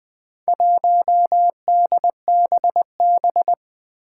Morse code, heard as a keyed tone, about 20 wpm, 700 Hz. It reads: 1DBB
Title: Four Challenging Letter-Number Combinations 20wpm